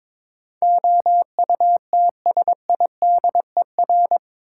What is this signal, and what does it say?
Morse code, heard as a keyed tone, about 21 wpm, 700 Hz. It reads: OUTSIDER
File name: practice / 22wpm / Eight Letter Words - 22wpm